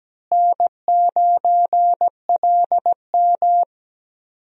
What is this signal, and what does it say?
Morse code, heard as a keyed tone, about 17 wpm, 700 Hz. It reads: N9LM
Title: USA Call Signs - 4 Characters - Repeated 3x - 17wpm